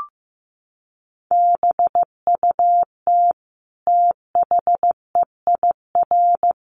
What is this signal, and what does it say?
Morse code, heard as a keyed tone, about 15 wpm, 700 Hz. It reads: BUT THEIR
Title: Sentences from Top 1400 Words 15wpm